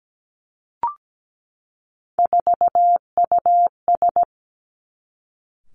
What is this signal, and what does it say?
Morse code, heard as a keyed tone, about 17 wpm, 700 Hz. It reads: 4US
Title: Three Challenging Letter-Number Combinations 17wpm